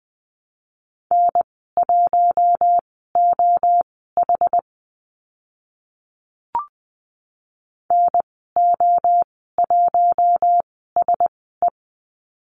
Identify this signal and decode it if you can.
Morse code, heard as a keyed tone, about 20 wpm, spaced as for 15 wpm, 700 Hz. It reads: N1OH NO1SE
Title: 062 Lesson 11g - Letter h - Call Signs 20wpm